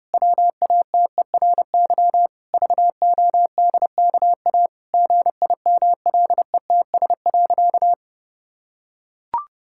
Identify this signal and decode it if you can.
Morse code, heard as a keyed tone, about 28 wpm, 700 Hz. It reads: WATERY VODKA GIMLETS.
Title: Pangram Challenge v2 - 30wpm